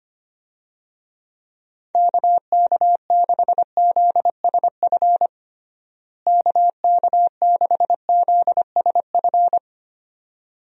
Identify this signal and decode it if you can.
Morse code, heard as a keyed tone, about 25 wpm, 700 Hz. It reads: KK6ZSF KK6ZSF
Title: USA Call Signs - 6 Characters - Repeated 2x - 25wpm